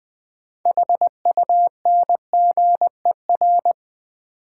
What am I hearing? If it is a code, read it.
Morse code, heard as a keyed tone, about 20 wpm, 700 Hz. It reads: HUNGER
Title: 6 Letter Words - First 3 Letters Given - 20wpm